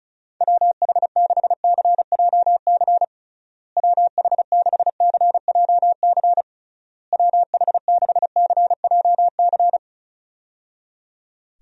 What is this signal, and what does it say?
Morse code, heard as a keyed tone, about 36 wpm, 700 Hz. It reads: WH6CJC WH6CJC WH6CJC
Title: USA Call Signs - 6 Characters - Repeated 3x - 35wpm